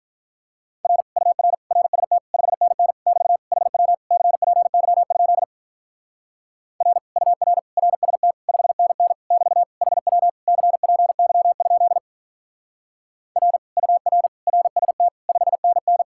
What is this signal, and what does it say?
Morse code, heard as a keyed tone, about 44 wpm, 700 Hz. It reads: R UR RST 5NN <BT> HW CPY? R UR RST 5NN <BT> HW CPY? R UR RST 5NN